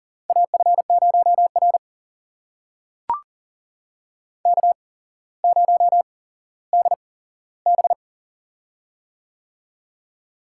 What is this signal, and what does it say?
Morse code, heard as a keyed tone, about 41 wpm, spaced as for 17 wpm, 700 Hz. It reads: AF0R K0DB
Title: USA Call Signs - 4 Characters - 8x Character Spacing - 40wpm